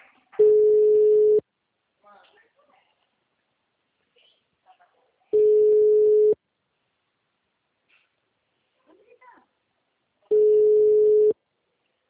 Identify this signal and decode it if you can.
Ringback tone (Europe, 425 Hz), with speech.
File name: out- -2016- - - .54